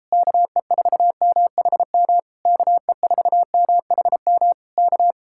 Morse code, heard as a keyed tone, about 33 wpm, 700 Hz. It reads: KE4MHM KE4MHM K